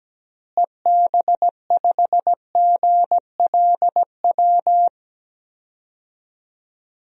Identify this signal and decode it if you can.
Morse code, heard as a keyed tone, about 17 wpm, 700 Hz. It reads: EB5GLW
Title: DXCC-100 Call Signs - 6 Characters - Repeated 3x - 17wpm